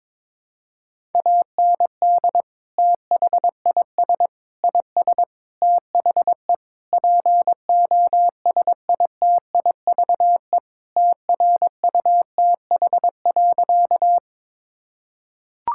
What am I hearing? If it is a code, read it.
Morse code, heard as a keyed tone, about 21 wpm, 700 Hz. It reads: AND THIS IS THE POSITIVE TRUTH.
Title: Sentences from Top 1000 Words - Review Part 5 - 22wpm